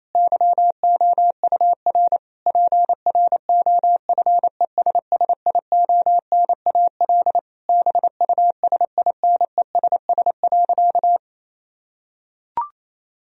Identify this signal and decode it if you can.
Morse code, heard as a keyed tone, about 28 wpm, 700 Hz. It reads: YOUR PROFESSIONAL BUSINESS.
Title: Sentences from Top 1000 Words - Review Part 4 - 28wpm